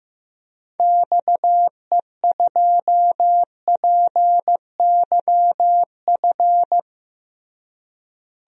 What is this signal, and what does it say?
Morse code, heard as a keyed tone, about 16 wpm, 700 Hz. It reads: XE2PYF
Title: DXCC-100 Call Signs - 6 Characters - Repeated 2x - 15wpm